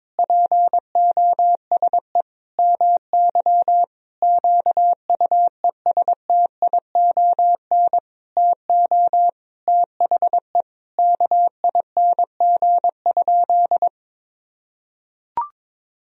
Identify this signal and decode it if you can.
Morse code, heard as a keyed tone, about 22 wpm, 700 Hz. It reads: POSE MY QUESTION TO THE KING?
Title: Sentences from Top 1000 Words - Review Part 2 - 22wpm